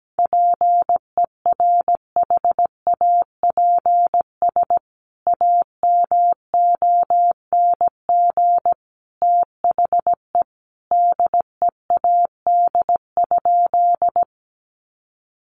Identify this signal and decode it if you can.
Morse code, heard as a keyed tone, about 17 wpm, 700 Hz. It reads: PERHAPS AMONG THE DEAD?
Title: Sentences from Top 1500 Words 17wpm